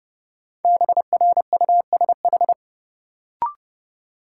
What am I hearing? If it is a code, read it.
Morse code, heard as a keyed tone, about 30 wpm, 700 Hz. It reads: BRUSH
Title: Sentences from Top 1100 Words - New Word Drill Alt - 30wpm